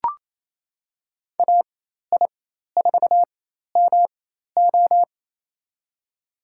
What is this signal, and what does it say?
Morse code, heard as a keyed tone, about 28 wpm, spaced as for 13 wpm, 700 Hz. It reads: AI4MO
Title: USA Call Signs - 5 Characters - 4x Character Spacing - 28wpm